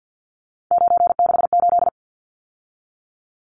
Morse code, heard as a keyed tone, about 50 wpm, 700 Hz. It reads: 968
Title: Numbers - 3-Digit 50wpm